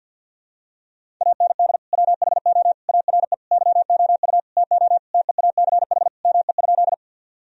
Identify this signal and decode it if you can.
Morse code, heard as a keyed tone, about 50 wpm, 700 Hz. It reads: AND WHO ARE YOU TO TEACH ME?
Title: Sentences from Top 1000 Words - Review Part 2 - 50wpm